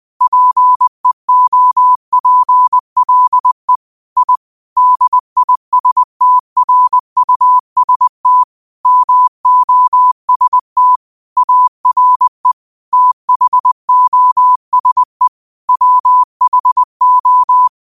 Morse code, heard as a keyed tone, about 20 wpm, 1000 Hz. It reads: PEOPLE I DISTRUST MOST ARE THOSE WHO
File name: qotdfiles / 20wpm